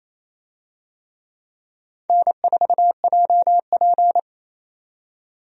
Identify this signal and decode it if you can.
Morse code, heard as a keyed tone, about 27 wpm, 700 Hz. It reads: N4JP
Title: USA Call Signs - 4 Characters - 28wpm